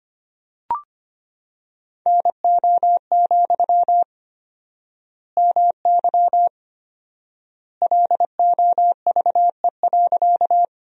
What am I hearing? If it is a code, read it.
Morse code, heard as a keyed tone, about 25 wpm, 700 Hz. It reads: NO, MY LOVE.
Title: 4x Word-Spacing - Sentences from Top 400 Words - 25wpm